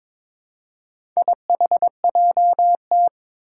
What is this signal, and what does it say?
Morse code, heard as a keyed tone, about 22 wpm, 700 Hz. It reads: IHJT
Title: Four Challenging Letter-Number Combinations 22wpm